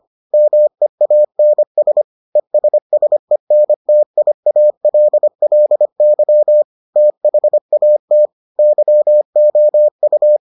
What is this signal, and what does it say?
Morse code, heard as a keyed tone, about 25 wpm, 600 Hz. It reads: MEANS ESSENTIALLY THAT YOU